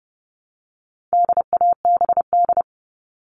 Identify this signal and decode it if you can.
Morse code, heard as a keyed tone, about 30 wpm, 700 Hz. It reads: DABD
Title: All Cut Numbers - 4-Digit 30wpm